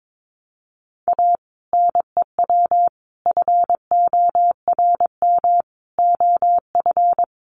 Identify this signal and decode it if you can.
Morse code, heard as a keyed tone, about 22 wpm, 700 Hz. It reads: A NEW FORM OF
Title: Sentences from Top 1300 Words 22wpm